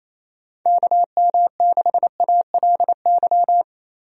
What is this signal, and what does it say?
Morse code, heard as a keyed tone, about 28 wpm, 700 Hz. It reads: KM6ALY
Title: USA Call Signs - 6 Characters - Repeated 2x - 28wpm